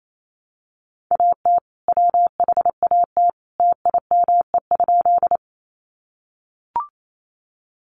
Morse code, heard as a keyed tone, about 28 wpm, 700 Hz. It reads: AT WHAT TIME?
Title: Sentences from Top 100 Words 28wpm